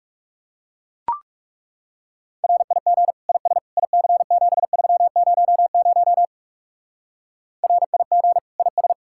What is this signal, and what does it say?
Morse code, heard as a keyed tone, about 36 wpm, 700 Hz. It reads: RIG IS IC7300 RIG IS
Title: Speed-Racing Common QSO Phrases Encore - 30wpm